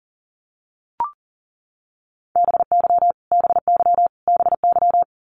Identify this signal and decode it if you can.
Morse code, heard as a keyed tone, about 40 wpm, 700 Hz. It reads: BY BY BY
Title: Top 200 Words Repeated 40wpm